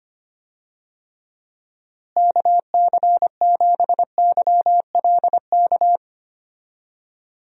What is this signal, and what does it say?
Morse code, heard as a keyed tone, about 25 wpm, 700 Hz. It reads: KC7YLK
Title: USA Call Signs - 6 Characters - Repeated 2x - 25wpm